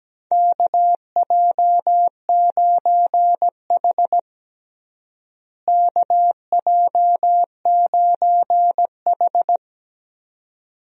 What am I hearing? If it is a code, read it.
Morse code, heard as a keyed tone, about 17 wpm, 700 Hz. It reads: KJ9H KJ9H
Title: USA Call Signs - 4 Characters - Repeated 2x - 17wpm